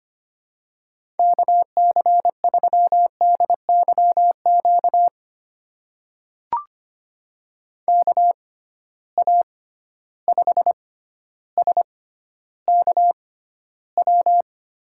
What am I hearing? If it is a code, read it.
Morse code, heard as a keyed tone, about 25 wpm, spaced as for 13 wpm, 700 Hz. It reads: KC3DYQ KA5SKW
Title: USA Call Signs - 6 Characters - 6x Character Spacing - 25wpm